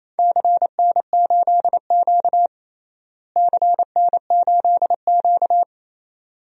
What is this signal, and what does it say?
Morse code, heard as a keyed tone, about 28 wpm, 700 Hz. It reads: CN8Q CN8Q
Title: DXCC-100 Call Signs - 4 Characters - Repeated 2x - 28wpm